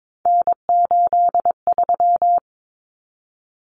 Morse code, heard as a keyed tone, about 22 wpm, 700 Hz. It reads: N83
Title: Common Cut Numbers - 3-Digit 22wpm